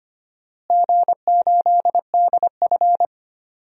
Morse code, heard as a keyed tone, about 25 wpm, 700 Hz. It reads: G8DF
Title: DXCC-100 Call Signs - 4 Characters - Repeated 3x - 25wpm